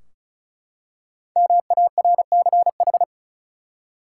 Morse code, heard as a keyed tone, about 34 wpm, 700 Hz. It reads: MARCH
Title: Five Letter Words - 35wpm